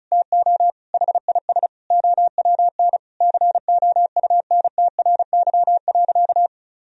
Morse code, heard as a keyed tone, about 34 wpm, 700 Hz. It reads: TO HIS OWN COUNTRY.
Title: Sentences from Top 1000 Words - Review Part 1 - 35wpm